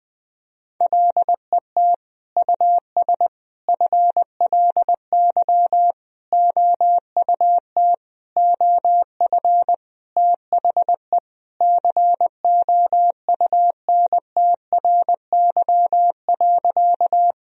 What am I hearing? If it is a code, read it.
Morse code, heard as a keyed tone, about 20 wpm, 700 Hz. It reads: LET US FLY OUT OF THE COUNTRY.
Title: Sentences from Top 1000 Words - Review Part 2 - 20wpm